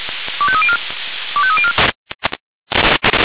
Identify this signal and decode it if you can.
Five-tone selective call (ZVEI), heard as quick runs of five tones, tones of about 70 ms. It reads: 25304 25304